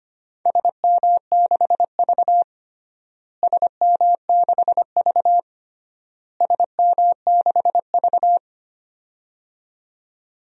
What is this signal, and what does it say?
Morse code, heard as a keyed tone, about 25 wpm, 700 Hz. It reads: SM6V SM6V SM6V
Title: DXCC-100 Call Signs - 4 Characters - Repeated 3x - 25wpm